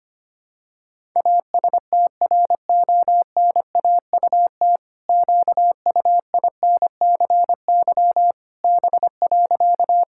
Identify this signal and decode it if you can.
Morse code, heard as a keyed tone, about 25 wpm, 700 Hz. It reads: ASTRONAUT QUINCY B.
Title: Pangram Challenge v2 - 25wpm